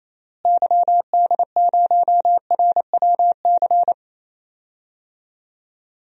Morse code, heard as a keyed tone, about 28 wpm, 700 Hz. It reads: YD0RWC
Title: DXCC-100 Call Signs - 6 Characters - Repeated 2x - 28wpm